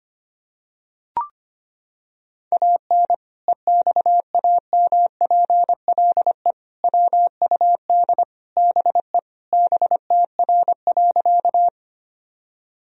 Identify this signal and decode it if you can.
Morse code, heard as a keyed tone, about 25 wpm, 700 Hz. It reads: AN EXAMPLE WUD BE BTR.
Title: Sentences from Top 400 Words and Abbreviations 25wpm